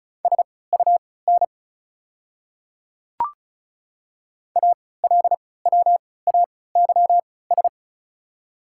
Morse code, heard as a keyed tone, about 36 wpm, spaced as for 20 wpm, 700 Hz. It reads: SUN ALWAYS
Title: 3x Character Spacing - Top 300 Words - 35wpm